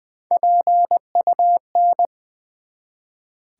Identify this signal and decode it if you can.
Morse code, heard as a keyed tone, about 20 wpm, 700 Hz. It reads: PUN
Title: Build A Word - 1 Letter At A Time - 5-Letter Words - 20wpm